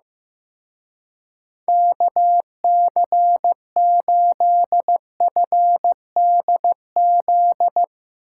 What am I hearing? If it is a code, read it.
Morse code, heard as a keyed tone, about 15 wpm, 700 Hz. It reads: KC8FDZ
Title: Speed Racing - USA 2x3 Call Signs 15wpm